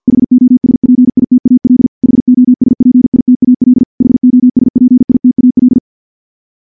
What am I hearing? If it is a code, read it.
Morse code, heard as a keyed tone, about 50 wpm, 250 Hz. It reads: HOSPITAL HOSPITAL HOSPITAL
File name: morse / morse-training-50WPM